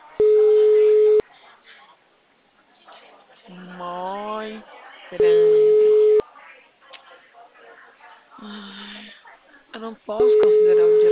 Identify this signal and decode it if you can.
Ringback tone (Europe, 425 Hz), with speech.